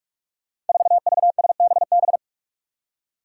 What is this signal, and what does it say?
Morse code, heard as a keyed tone, about 44 wpm, 700 Hz. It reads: 4VSBB
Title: Five Challenging Letter-Number Combinations 45wpm